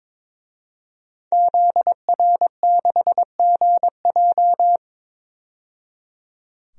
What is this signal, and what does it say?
Morse code, heard as a keyed tone, about 22 wpm, 700 Hz. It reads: ZR6GJ
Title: DXCC-100 Call Signs - 5 Characters - 22wpm